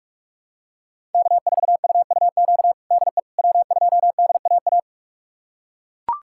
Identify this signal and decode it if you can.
Morse code, heard as a keyed tone, about 41 wpm, 700 Hz. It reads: K4UUQ DE W2DAA
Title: USA Callsign DE Callsign - 5 Characters - 45wpm